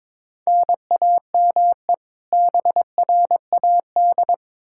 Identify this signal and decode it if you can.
Morse code, heard as a keyed tone, about 22 wpm, 700 Hz. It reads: NAME BRAD
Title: Practice QSO - Exchange 1 - Beginning - 22wpm